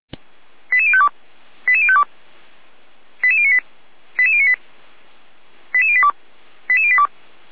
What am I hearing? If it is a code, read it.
Five-tone selective call (ZVEI), heard as quick runs of five tones, tones of about 70 ms. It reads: 88052 88052 88098 88098 88082 88082